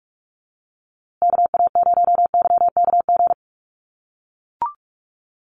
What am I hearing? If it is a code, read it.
Morse code, heard as a keyed tone, about 44 wpm, 700 Hz. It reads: KA0YKG
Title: USA Call Signs Repeated 3x 45wpm